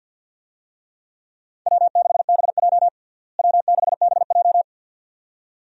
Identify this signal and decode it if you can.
Morse code, heard as a keyed tone, about 48 wpm, 700 Hz. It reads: W6BJ W6BJ
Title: USA Call Signs - 4 Characters - Repeated 2x - 50wpm